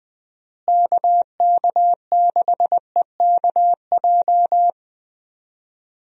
Morse code, heard as a keyed tone, about 20 wpm, 700 Hz. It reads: KK6EKJ
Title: USA Call Signs - 6 Characters - Repeated 3x - 20wpm